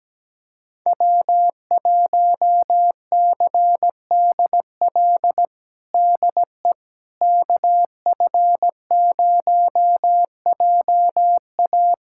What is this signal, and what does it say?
Morse code, heard as a keyed tone, about 17 wpm, 700 Hz. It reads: W1CDL DE KF0JA
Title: USA Callsign DE Callsign - 5 Characters - 17wpm